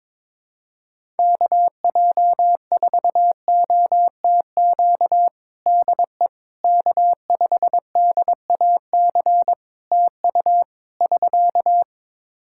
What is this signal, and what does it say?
Morse code, heard as a keyed tone, about 22 wpm, 700 Hz. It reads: KJ4OTQ DE K5DAC TU <SK>
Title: Practice QSO - Exchange 4 - Ending - 22wpm